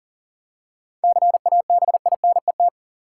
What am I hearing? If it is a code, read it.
Morse code, heard as a keyed tone, about 37 wpm, 700 Hz. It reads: CABINET
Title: Seven Letter Words - 40wpm